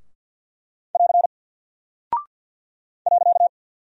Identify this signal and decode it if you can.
Morse code, heard as a keyed tone, about 48 wpm, 700 Hz. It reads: <AR> .
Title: Single Character (Letter:Number:Punctuation:Prosign) 50wpm